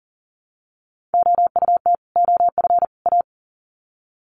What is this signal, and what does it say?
Morse code, heard as a keyed tone, about 40 wpm, 700 Hz. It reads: OUT OF A
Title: Sets of 3 Words 40wpm